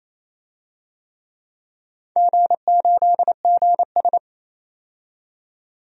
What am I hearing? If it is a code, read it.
Morse code, heard as a keyed tone, about 28 wpm, 700 Hz. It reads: G8GS
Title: DXCC-100 Call Signs - 4 Characters - 28wpm